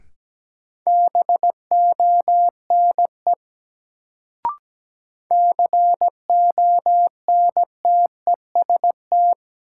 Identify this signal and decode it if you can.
Morse code, heard as a keyed tone, about 17 wpm, 700 Hz. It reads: BONE CONTEST
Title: Sentences from Top 1600 Words - New Word Drill Alt - 17wpm